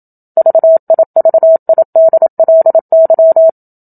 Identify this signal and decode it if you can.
Morse code, heard as a keyed tone, about 27 wpm, 650 Hz. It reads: VIVIDLY
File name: Book Two / 01-Chapter 1 - 27 WPM